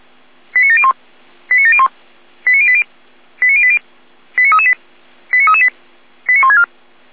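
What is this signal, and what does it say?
Five-tone selective call (ZVEI), heard as quick runs of five tones, tones of about 70 ms. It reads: 89701 89701 89080 89080 89338 89338 89164
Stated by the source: ZVEI five-tone selective call received by SDR.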